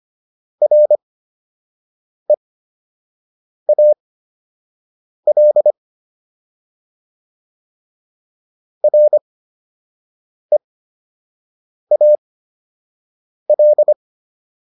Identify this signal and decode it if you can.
Morse code, heard as a keyed tone, about 25 wpm, spaced as for 6 wpm, 600 Hz. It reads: REAL REAL